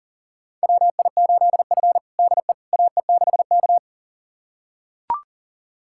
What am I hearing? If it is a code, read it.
Morse code, heard as a keyed tone, about 40 wpm, 700 Hz. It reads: WI8F DE AE6K
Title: USA Callsign DE Callsign - 4 Characters - 40wpm